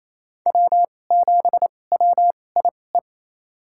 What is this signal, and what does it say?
Morse code, heard as a keyed tone, about 28 wpm, spaced as for 20 wpm, 700 Hz. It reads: W7WIE
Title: USA Call Signs - 5 Characters - 2x Character Spacing - 28wpm